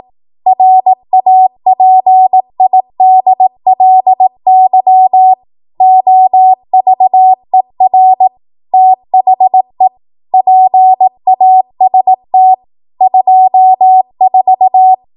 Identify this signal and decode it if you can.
Morse code, heard as a keyed tone, about 18 wpm, 750 Hz. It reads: RAPIDLY OVER THE PAST 24
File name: Archive / 18 WPM / 18WPM